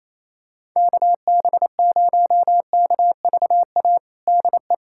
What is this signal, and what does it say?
Morse code, heard as a keyed tone, about 28 wpm, 700 Hz. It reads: KB0KVA DE
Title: USA Callsign DE Callsign - 6 Characters - 28wpm